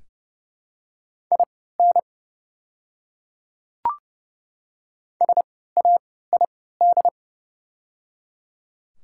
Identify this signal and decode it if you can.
Morse code, heard as a keyed tone, about 30 wpm, spaced as for 17 wpm, 700 Hz. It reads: IN SAID
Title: 3x Character Spacing - Top 100 Words - 30wpm